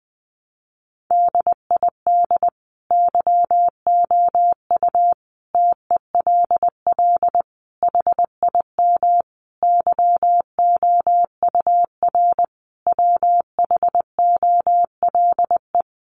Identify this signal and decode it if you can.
Morse code, heard as a keyed tone, about 20 wpm, 700 Hz. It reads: DID YOU TELL HIM YOUR WHOLE
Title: Sentences from Top 400 Words 20wpm